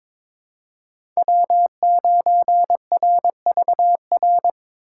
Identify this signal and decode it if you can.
Morse code, heard as a keyed tone, about 22 wpm, 700 Hz. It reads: W9RVR
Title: USA Call Signs - 6x Character Spacing - 22wpm